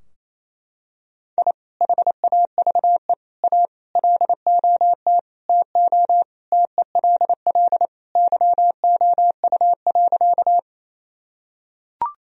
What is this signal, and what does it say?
Morse code, heard as a keyed tone, about 27 wpm, 700 Hz. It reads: I HAVE A LOT TO TELL YOU.
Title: Sentences from Top 1000 Words - Review Part 3 - 28wpm